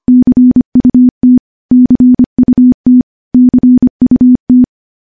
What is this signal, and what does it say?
Morse code, heard as a keyed tone, about 25 wpm, 250 Hz. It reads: CUT CUT CUT
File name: morse / morse-training-25WPM